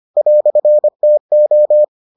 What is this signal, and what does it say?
Morse code, heard as a keyed tone, about 25 wpm, 600 Hz. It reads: "TO